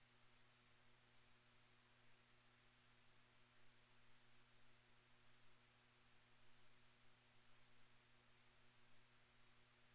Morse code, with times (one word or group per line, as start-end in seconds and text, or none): none